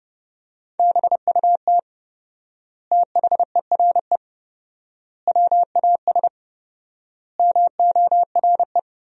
Morse code, heard as a keyed tone, about 30 wpm, 700 Hz. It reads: BUT THERE WAS MORE
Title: 4x Word-Spacing - Sentences from Top 100 Words - 30wpm